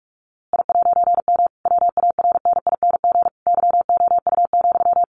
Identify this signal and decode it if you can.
Morse code, heard as a keyed tone, about 44 wpm, 700 Hz. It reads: I'M WARNING YOU,
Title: A Sound Of Thunder 45wpm